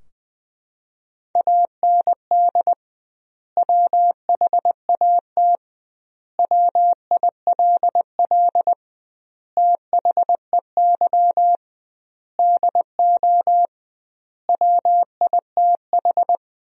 Morse code, heard as a keyed tone, about 20 wpm, 700 Hz. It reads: AND WHAT WILL THEY DO WITH
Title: 2x Word-Spacing - Sentences from Top 200 Words - 20wpm